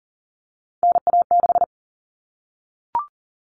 Morse code, heard as a keyed tone, about 40 wpm, 700 Hz. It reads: NA6
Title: Common Cut Numbers - 3-Digit 40wpm